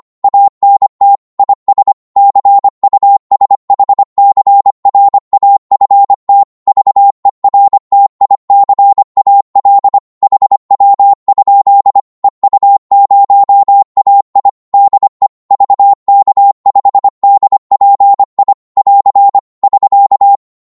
Morse code, heard as a keyed tone, about 25 wpm, 800 Hz. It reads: ANT IS CUSHCRAFT VERTICAL HW? EU0AI DE VK5DPI <AR> <SK>
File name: Morse / qso-25wpm